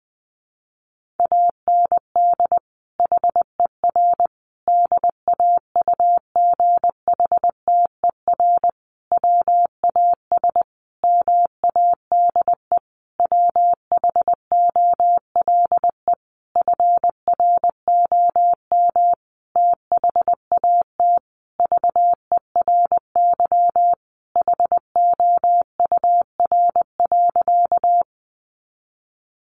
Morse code, heard as a keyed tone, about 20 wpm, 700 Hz. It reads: AND HER DAUGHTER WAS MADE WHOLE FROM THAT VERY HOUR.
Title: Sentences from Top 1500 Words 20wpm